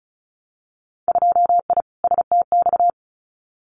Morse code, heard as a keyed tone, about 34 wpm, 690 Hz. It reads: 2I STX
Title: Winter Field Day Exchanges 35wpm